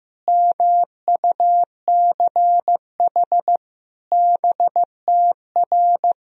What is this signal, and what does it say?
Morse code, heard as a keyed tone, about 16 wpm, 700 Hz. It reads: MUCH BTR
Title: Sentences from Top 400 Words and Abbreviations 15wpm